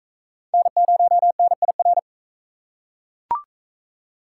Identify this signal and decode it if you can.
Morse code, heard as a keyed tone, about 41 wpm, 700 Hz. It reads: N0NIR